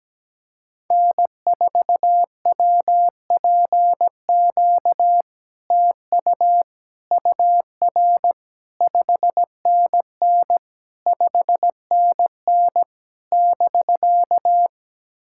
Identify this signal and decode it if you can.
Morse code, heard as a keyed tone, about 17 wpm, 700 Hz. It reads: N4WPQ TU UR 5NN 5NN <BK>
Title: SOTA QSO Practice 17wpm